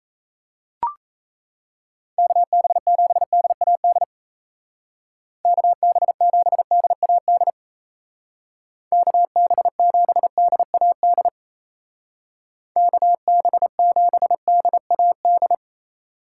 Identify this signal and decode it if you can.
Morse code, heard as a keyed tone, about 34 wpm, 700 Hz. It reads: KB7DAD KB7DAD KB7DAD KB7DAD
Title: Speed Racing - USA 2x3 Call Signs 28wpm